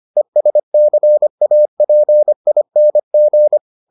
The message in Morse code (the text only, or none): ESCAPING